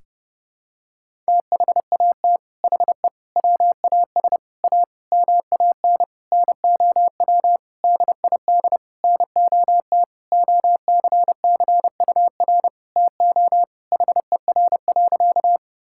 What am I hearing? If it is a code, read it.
Morse code, heard as a keyed tone, about 32 wpm, 700 Hz. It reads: THAT HE WAS A MAN NOW DID NOT OCCUR TO HER.
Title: Sentences from Top 1000 Words - Review Part 6 - 30wpm